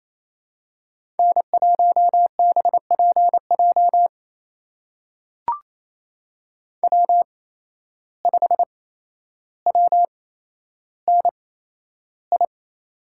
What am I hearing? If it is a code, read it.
Morse code, heard as a keyed tone, about 28 wpm, spaced as for 12 wpm, 700 Hz. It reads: N1BPJ W5WNI
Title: USA Call Signs - 5 Characters - 8x Character Spacing - 28wpm